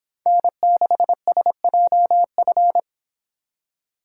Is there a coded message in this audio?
Yes: Morse code from 0.3 to 2.8 s, heard as a keyed tone, about 26 wpm, 700 Hz. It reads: N6SJF